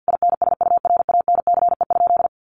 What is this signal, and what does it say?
Morse code, heard as a keyed tone, about 50 wpm, 700 Hz. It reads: INSURANCE?
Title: Sentences from Top 1100 Words 50wpm